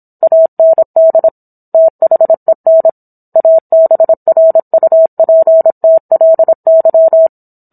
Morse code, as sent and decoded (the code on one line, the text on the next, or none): .- -. -.. / - .... -. / .- -... .-. ..- .--. - .-.. -.--
AND THEN ABRUPTLY